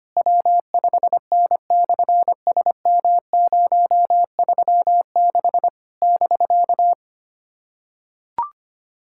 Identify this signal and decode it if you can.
Morse code, heard as a keyed tone, about 25 wpm, 700 Hz. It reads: W5N/SM036 <BK>
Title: SOTA QSO Practice 25wpm